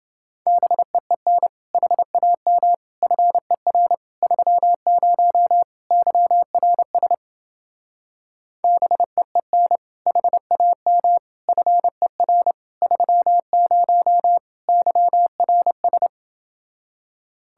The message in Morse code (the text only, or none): BEEN HAM FER 30 YRS BEEN HAM FER 30 YRS